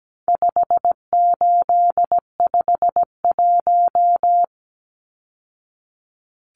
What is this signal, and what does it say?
Morse code, heard as a keyed tone, about 17 wpm, 700 Hz. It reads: 5851
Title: Numbers - 4-Digit 17wpm